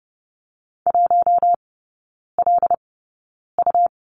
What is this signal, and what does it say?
Morse code, heard as a keyed tone, about 30 wpm, spaced as for 9 wpm, 700 Hz. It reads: 1LU
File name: Lesson 17e - Letter u - Three Characters 30wpm